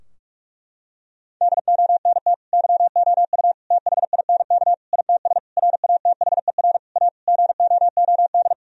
Morse code, heard as a keyed tone, about 46 wpm, 700 Hz. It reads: DONT YOU THINK ITS RATHER A GOOD
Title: Sentences from Top 1000 Words - Review Part 4 - 45wpm